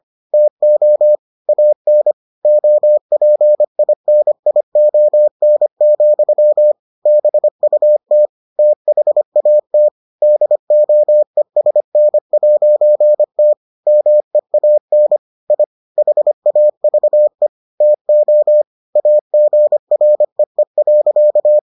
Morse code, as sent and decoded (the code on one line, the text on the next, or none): - --- / .- -. / --- .--. .. -. .. --- -. --..-- / -... ..- - / - .... .- - / -.. --- . ... -. .----. - / -- . .- -. / .. / .... .- ...- . / - --- / .- --. .-. . .-.-.-
TO AN OPINION, BUT THAT DOESN'T MEAN I HAVE TO AGREE.